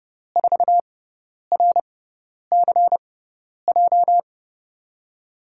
Morse code, heard as a keyed tone, about 32 wpm, spaced as for 10 wpm, 700 Hz. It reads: 4RCJ
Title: USA Call Signs - 6 Characters - 6x Character Spacing - 30wpm